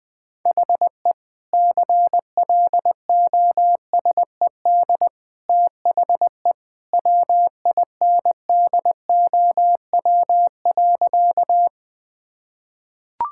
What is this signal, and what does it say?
Morse code, heard as a keyed tone, about 20 wpm, 700 Hz. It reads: HE CLOSED THE WINDOW.
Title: Sentences from Top 1600 Words 20wpm